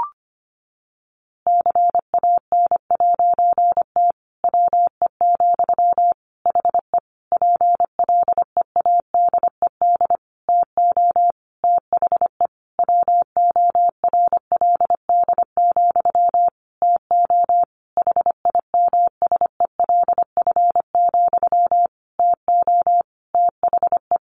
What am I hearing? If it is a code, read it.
Morse code, heard as a keyed tone, about 25 wpm, 700 Hz. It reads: CAN'T WE, HE PLEADED TO THE WORLD, TO HIMSELF, TO THE
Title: A Sound Of Thunder 25wpm